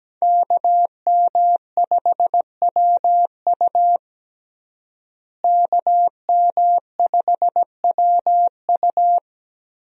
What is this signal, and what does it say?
Morse code, heard as a keyed tone, about 17 wpm, 700 Hz. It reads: KM5WU KM5WU
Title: USA Call Signs - 5 Characters - Repeated 2x - 17wpm